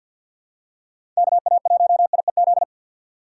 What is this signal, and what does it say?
Morse code, heard as a keyed tone, about 51 wpm, 700 Hz. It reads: KA1IEZ
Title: USA Call Signs - 2x Character Spacing - 50wpm